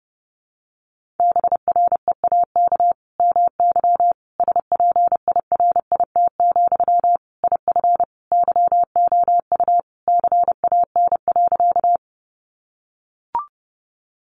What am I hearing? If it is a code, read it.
Morse code, heard as a keyed tone, about 28 wpm, 700 Hz. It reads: BREAK MY SPIRIT, IF YOU CAN.
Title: Sentences from Top 1400 Words 30wpm